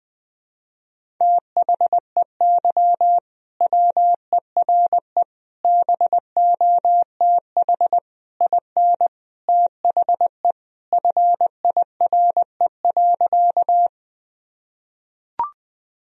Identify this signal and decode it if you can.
Morse code, heard as a keyed tone, about 20 wpm, 700 Hz. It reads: THEY WERE BOTH IN THE FIRE.
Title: Sentences from Top 1000 Words - Review Part 1 - 20wpm